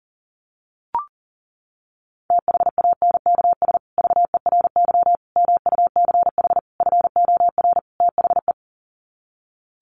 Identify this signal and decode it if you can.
Morse code, heard as a keyed tone, about 40 wpm, 700 Hz. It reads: THANKS VERY MUCH FOR THE
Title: Sets of 5 Words Encore2 40wpm